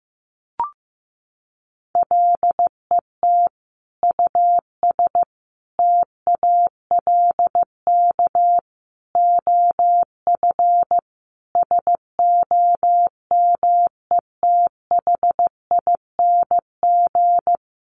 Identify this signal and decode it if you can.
Morse code, heard as a keyed tone, about 14 wpm, 700 Hz. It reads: LET US TALK OF SOMETHING
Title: Sentences from Top 800 Words 15wpm